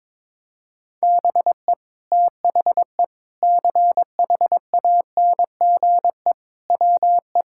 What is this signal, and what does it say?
Morse code, heard as a keyed tone, about 21 wpm, 700 Hz. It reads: BE THE CHANGE WE
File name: Sentences with 9-Word Phrases - 22wpm